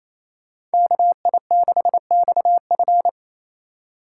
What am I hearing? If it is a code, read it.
Morse code, heard as a keyed tone, about 37 wpm, 700 Hz. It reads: KI6XF